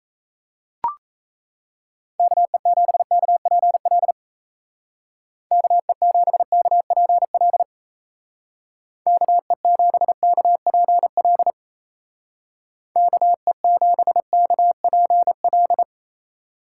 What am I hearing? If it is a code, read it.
Morse code, heard as a keyed tone, about 35 wpm, 700 Hz. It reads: KE7KPL KE7KPL KE7KPL KE7KPL
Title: Speed Racing - USA Call Signs 28wpm